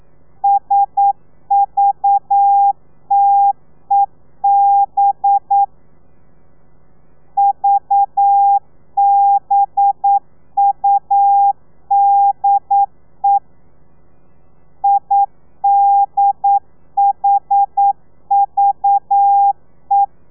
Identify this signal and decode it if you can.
Morse code, heard as a keyed tone, about 9 wpm, 800 Hz. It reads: SVTEB VBUDE IDHVE
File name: telegrafia / i2tfj / Lez4 40